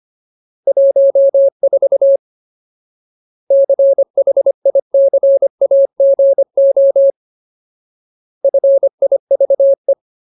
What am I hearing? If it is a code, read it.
Morse code, heard as a keyed tone, about 25 wpm, 550 Hz. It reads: 14 CHICAGO FIVE
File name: Ultimate Word List / 25WPM Ultimate